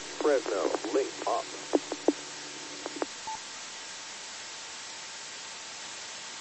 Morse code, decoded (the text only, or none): none